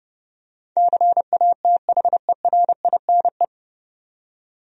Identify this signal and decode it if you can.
Morse code, heard as a keyed tone, about 30 wpm, 700 Hz. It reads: CATHERINE
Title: USA Common Names - 30wpm